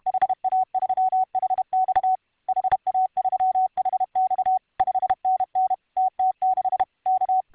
Morse code, heard as a keyed tone, about 31 wpm, 750 Hz. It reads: HA3HX HA3HX 5NN TT6 K